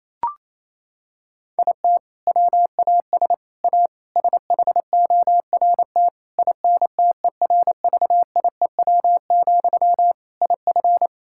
Morse code, heard as a keyed tone, about 28 wpm, 700 Hz. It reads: IT WAS A SHORT INTERVIEW, IF
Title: Sentences from Top 1600 Words 28wpm